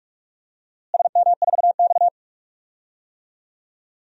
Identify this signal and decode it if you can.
Morse code, heard as a keyed tone, about 43 wpm, 700 Hz. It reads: SM4X
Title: DXCC-100 Call Signs - 4 Characters - Repeated 3x - 45wpm